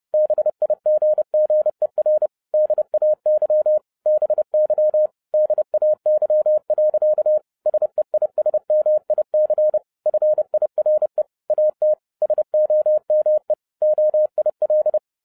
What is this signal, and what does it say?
Morse code, heard as a keyed tone, about 30 wpm, 610 Hz. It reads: BIGGER DAY BY DAY. SEISMIC FIRE AT SOME OIL